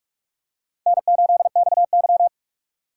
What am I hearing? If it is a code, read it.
Morse code, heard as a keyed tone, about 41 wpm, 700 Hz. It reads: N8XY
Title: USA Call Signs - 4x Character Spacing - 45wpm